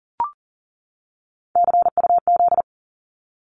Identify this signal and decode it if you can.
Morse code, heard as a keyed tone, about 40 wpm, 700 Hz. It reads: CUZ
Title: Abbreviations 40wpm